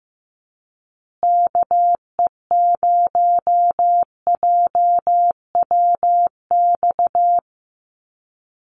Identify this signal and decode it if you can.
Morse code, heard as a keyed tone, about 15 wpm, 700 Hz. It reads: KE0JWX
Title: USA Call Signs Repeated 2x 15wpm